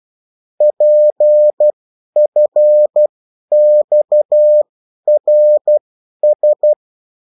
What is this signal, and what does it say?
Morse code, heard as a keyed tone, about 12 wpm, spaced as for 10 wpm, 600 Hz. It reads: PFXRS
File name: morse mp3 / revision9-vitesse 10 mots